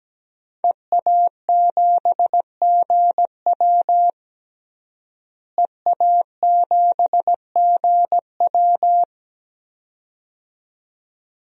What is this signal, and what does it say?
Morse code, heard as a keyed tone, about 17 wpm, 700 Hz. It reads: EA7GW EA7GW
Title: DXCC-100 Call Signs - 5 Characters - Repeated 2x - 17wpm